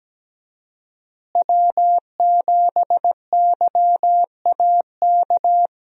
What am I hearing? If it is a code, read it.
Morse code, heard as a keyed tone, about 17 wpm, 700 Hz. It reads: W7YAK
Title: USA Call Signs - 5 Characters - Repeated 3x - 17wpm